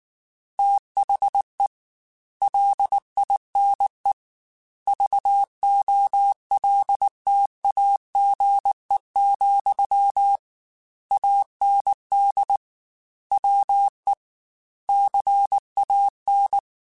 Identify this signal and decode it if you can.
Morse code, heard as a keyed tone, about 19 wpm, 800 Hz. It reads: THE LINE VOLTAGE, AND WE CAN
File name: Technical Subjects / Technical Subjects-mp3s / Sw Pwr Sup @18